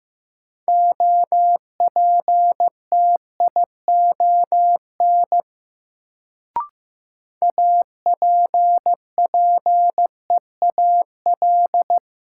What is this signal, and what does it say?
Morse code, heard as a keyed tone, about 15 wpm, 700 Hz. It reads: OPTION APPEAL
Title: Sentences from Top 1300 Words - New Word Drill - 15wpm